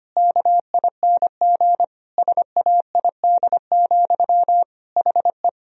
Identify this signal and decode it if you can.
Morse code, heard as a keyed tone, about 25 wpm, 700 Hz. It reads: KING SAID, HE
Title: Sentences from Top 1000 Words - Review Part 3 - 25wpm